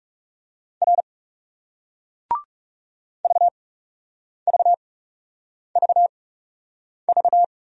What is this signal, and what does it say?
Morse code, heard as a keyed tone, about 40 wpm, spaced as for 8 wpm, 700 Hz. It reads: R VVVV